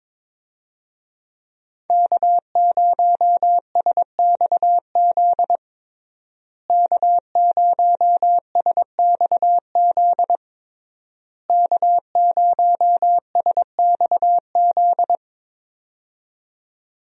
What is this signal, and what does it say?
Morse code, heard as a keyed tone, about 22 wpm, 700 Hz. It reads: K0SXZ K0SXZ K0SXZ
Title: USA Call Signs - 5 Characters - Repeated 3x - 22wpm